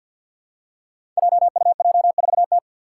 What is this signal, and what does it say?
Morse code, heard as a keyed tone, about 53 wpm, 700 Hz. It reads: JUJ4T
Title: Five Challenging Letter-Number Combinations 50wpm